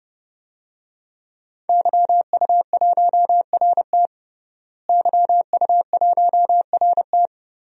Morse code, heard as a keyed tone, about 30 wpm, 700 Hz. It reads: YU1RT YU1RT
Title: DXCC-100 Call Signs - 5 Characters - Repeated 3x - 30wpm